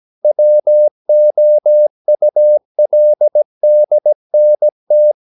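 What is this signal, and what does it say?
Morse code, heard as a keyed tone, about 17 wpm, 600 Hz. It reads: WOULDNT
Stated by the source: 17 wpm Morse practice sending the English text of a novel.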